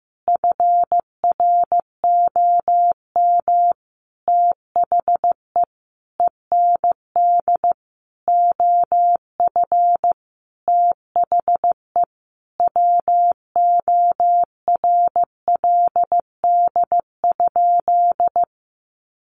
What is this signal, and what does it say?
Morse code, heard as a keyed tone, about 15 wpm, 700 Hz. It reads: FROM THE END OF THE WORLD?
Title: Sentences from Top 200 Words 15wpm